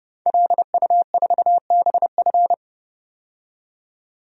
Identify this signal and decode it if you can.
Morse code, heard as a keyed tone, about 31 wpm, 700 Hz. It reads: LU4BF
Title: DXCC-100 Call Signs - 5 Characters - 30wpm